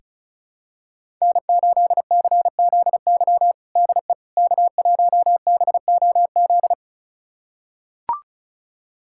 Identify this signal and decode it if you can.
Morse code, heard as a keyed tone, about 36 wpm, 690 Hz. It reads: N8CZY DE K1BOZ
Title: USA Callsign DE Callsign - 5 Characters - 35wpm